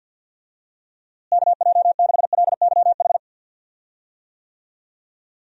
Morse code, heard as a keyed tone, about 47 wpm, 700 Hz. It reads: KJ6LYH
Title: USA Call Signs - 6 Characters - Repeated 2x - 50wpm